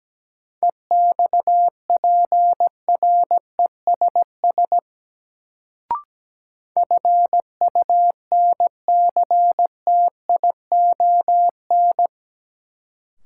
Morse code, heard as a keyed tone, about 17 wpm, 700 Hz. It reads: EXPRESS FUNCTION
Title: Sentences from Top 1100 Words - New Word Drill Alt - 17wpm